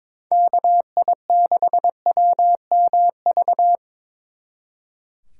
Morse code, heard as a keyed tone, about 25 wpm, 700 Hz. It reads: KI6WMV
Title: Speed Racing - USA 2x3 Call Signs 22wpm